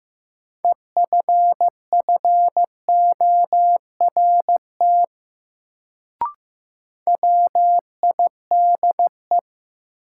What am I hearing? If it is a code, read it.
Morse code, heard as a keyed tone, about 15 wpm, 700 Hz. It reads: EFFORT WIDE
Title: Sets of 4 Words - New Word Drill - 15wpm